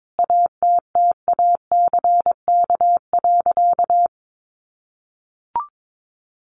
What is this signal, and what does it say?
Morse code, heard as a keyed tone, about 22 wpm, 700 Hz. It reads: ATTACK.
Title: Sentences from Top 1500 Words 22wpm